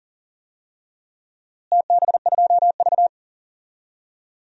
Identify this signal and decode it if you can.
Morse code, heard as a keyed tone, about 39 wpm, 700 Hz. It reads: TB2V
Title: DXCC-100 Call Signs - 4 Characters - Repeated 3x - 40wpm